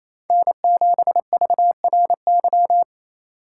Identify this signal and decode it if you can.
Morse code, heard as a keyed tone, about 36 wpm, 700 Hz. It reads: N7VRY